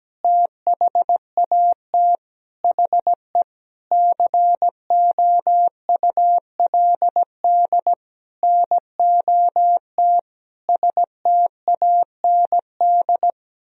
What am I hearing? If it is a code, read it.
Morse code, heard as a keyed tone, about 17 wpm, 700 Hz. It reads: THAT HE COULD NOT STAND
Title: Sentences from Top 1000 Words - Review Part 3 - 17wpm